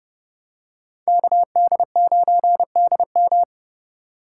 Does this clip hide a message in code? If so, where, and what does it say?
Yes: Morse code from 1.1 to 3.4 s, heard as a keyed tone, about 28 wpm, 700 Hz. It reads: KD9DM